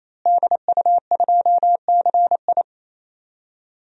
Morse code, heard as a keyed tone, about 28 wpm, 700 Hz. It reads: DU2CI
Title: DXCC-100 Call Signs - 5 Characters - Repeated 3x - 28wpm